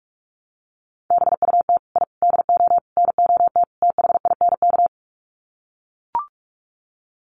Morse code, heard as a keyed tone, about 44 wpm, 700 Hz. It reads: BUT I DO NOT THINK
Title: Sets of 5 Words 45wpm